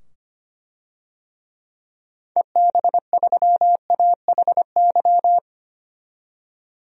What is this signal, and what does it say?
Morse code, heard as a keyed tone, about 25 wpm, 700 Hz. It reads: EB3AHY
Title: DXCC-100 Call Signs - 6 Characters - 25wpm